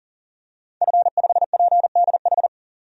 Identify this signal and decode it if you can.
Morse code, heard as a keyed tone, about 40 wpm, 700 Hz. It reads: F5PDH
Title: DXCC-100 Call Signs - 5 Characters - 40wpm